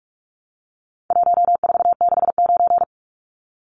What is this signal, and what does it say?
Morse code, heard as a keyed tone, about 44 wpm, 700 Hz. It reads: 1469
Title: Numbers - 4-Digit 45wpm